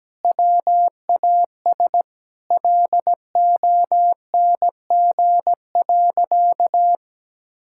Morse code, heard as a keyed tone, about 17 wpm, 700 Hz. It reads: WAS LONG.
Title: Sentences from Top 1000 Words - Review Part 6 - 17wpm